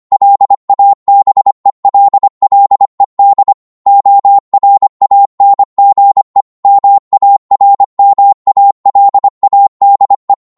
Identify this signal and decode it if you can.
Morse code, heard as a keyed tone, about 25 wpm, 800 Hz. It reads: LABELLED ORANGE MARMALADE